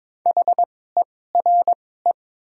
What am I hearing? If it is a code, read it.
Morse code, heard as a keyed tone, about 22 wpm, spaced as for 16 wpm, 690 Hz. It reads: HERE
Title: 2x Character Spacing - Top 200 Words - 22wpm